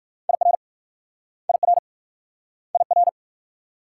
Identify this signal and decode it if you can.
Morse code, heard as a keyed tone, about 56 wpm, 700 Hz. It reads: IR IR IR
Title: Speed-Racing Multi-letter Phonemes - 45wpm